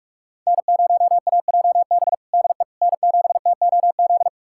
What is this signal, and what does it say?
Morse code, heard as a keyed tone, about 44 wpm, 700 Hz. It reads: N0AJB DE N7TOZ
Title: USA Callsign DE Callsign - 5 Characters - 45wpm